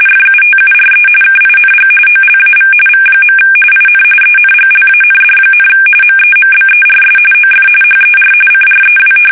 RTTY, about 45.45 baud. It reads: FUTURE KING OLAV V. IN ENGLAND, MEMBERS OF HER CREW
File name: fsk850 / battleshipnj